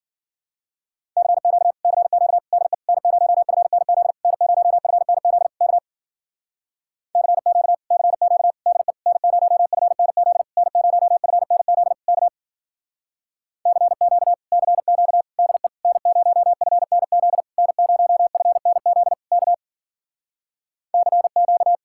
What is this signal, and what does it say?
Morse code, heard as a keyed tone, about 48 wpm, 700 Hz. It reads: CQ CQ DE N0FNZ N0FNZ K CQ CQ DE N0FNZ N0FNZ K CQ CQ DE N0FNZ N0FNZ K CQ
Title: Speed-Racing Common QSO Phrases - 40wpm